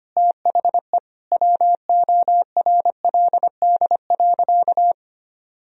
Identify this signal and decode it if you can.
Morse code, heard as a keyed tone, about 25 wpm, 700 Hz. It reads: THE WORLD.
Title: Sentences from Top 1000 Words - Review Part 1 - 25wpm